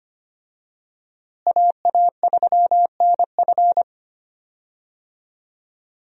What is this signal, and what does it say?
Morse code, heard as a keyed tone, about 25 wpm, 700 Hz. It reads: AA3NF
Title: USA Call Signs - 5 Characters - Repeated 3x - 25wpm